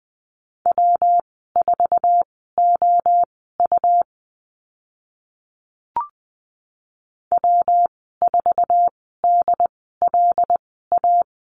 Morse code, heard as a keyed tone, about 20 wpm, spaced as for 14 wpm, 700 Hz. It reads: W4OU W4DLA
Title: 161 Lesson 26g - Letter g - Call Signs 20wpm